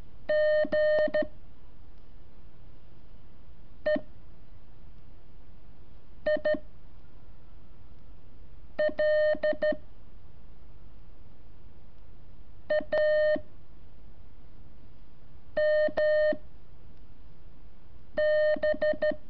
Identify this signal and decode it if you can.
Morse code, heard as a keyed tone, about 13 wpm, spaced as for 4 wpm, 610 Hz. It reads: G E I L A MB
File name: curso / ejercicio 23